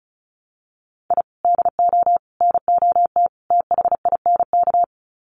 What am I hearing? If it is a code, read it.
Morse code, heard as a keyed tone, about 34 wpm, 700 Hz. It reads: I DO NOT THINK
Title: Sets of 4 Words 35wpm